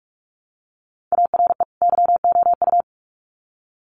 Morse code, heard as a keyed tone, about 44 wpm, 700 Hz. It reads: ARE YOU
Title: Sets of 2 Words 45wpm